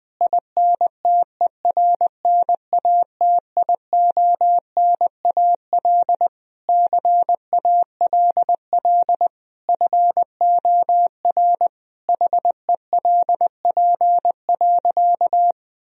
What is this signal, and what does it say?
Morse code, heard as a keyed tone, about 21 wpm, 700 Hz. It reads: INTERNATIONAL CALL FOR HELP.
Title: Sentences from Top 1000 Words - Review Part 4 - 20wpm